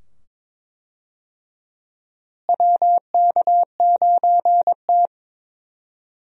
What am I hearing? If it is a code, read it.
Morse code, heard as a keyed tone, about 21 wpm, 700 Hz. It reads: WK9T
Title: USA Call Signs - 4 Characters - Repeated 2x - 22wpm